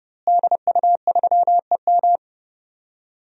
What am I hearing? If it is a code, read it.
Morse code, heard as a keyed tone, about 30 wpm, 700 Hz. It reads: DU3EM
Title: DXCC-100 Call Signs - 5 Characters - Repeated 3x - 30wpm